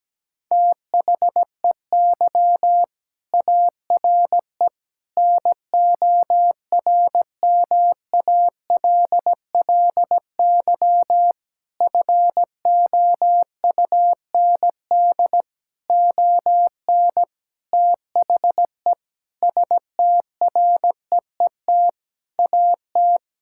Morse code, heard as a keyed tone, about 17 wpm, 700 Hz. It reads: THEY ARE NORMALLY FOUND ON THE STREET AT
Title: Sentences from Top 1000 Words - Review Part 6 - 17wpm